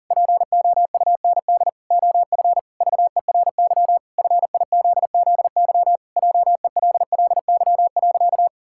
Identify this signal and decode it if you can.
Morse code, heard as a keyed tone, about 40 wpm, 700 Hz. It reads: POUND OF VERY FIZZY JELLY.
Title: Pangram Challenge v2 - 40wpm